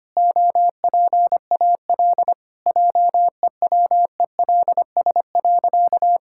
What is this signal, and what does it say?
Morse code, heard as a keyed tone, about 25 wpm, 700 Hz. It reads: OPAL JEWELS.
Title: Pangram Challenge v2 - 25wpm